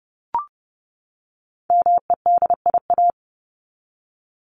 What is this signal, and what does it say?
Morse code, heard as a keyed tone, about 30 wpm, 700 Hz. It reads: MEDIA
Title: Top 501-750 Words 30wpm